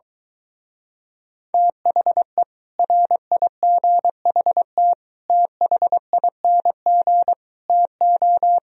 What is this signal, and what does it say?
Morse code, heard as a keyed tone, about 23 wpm, 700 Hz. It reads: THE RIGHT THING TO